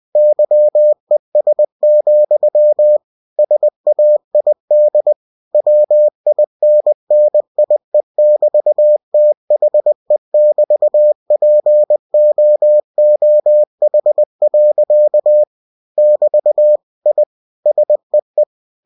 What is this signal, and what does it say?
Morse code, heard as a keyed tone, about 20 wpm, 600 Hz. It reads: YES, SAID WINNIE<BT>THE<BT>POOH. <BT> I SEE
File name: cw-books / winnie-the-pooh / winnie-20wpm-page-0025